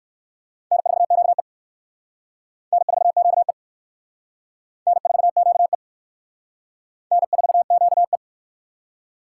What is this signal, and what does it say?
Morse code, heard as a keyed tone, about 55 wpm, 700 Hz. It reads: N4QE N4QE N4QE N4QE
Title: Speed Racing - USA 1x2 Call Signs 45wpm